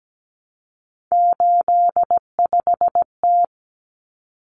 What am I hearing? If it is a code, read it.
Morse code, heard as a keyed tone, about 17 wpm, 700 Hz. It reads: 85T
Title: Common Cut Numbers - 3-Digit 17wpm